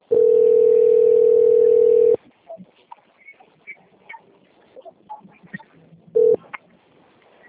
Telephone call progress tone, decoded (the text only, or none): ringback tone